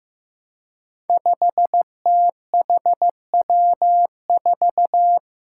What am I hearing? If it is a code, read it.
Morse code, heard as a keyed tone, about 15 wpm, 700 Hz. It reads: 5THW4
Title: Five Challenging Letter-Number Combinations 15wpm